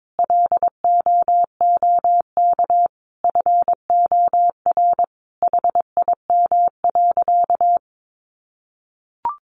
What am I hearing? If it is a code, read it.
Morse code, heard as a keyed tone, about 22 wpm, 700 Hz. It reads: LOOK FOR HIM.
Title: Sentences from Top 1100 Words 22wpm